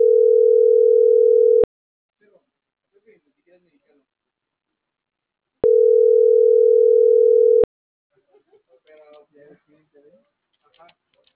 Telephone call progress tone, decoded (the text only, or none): ringback tone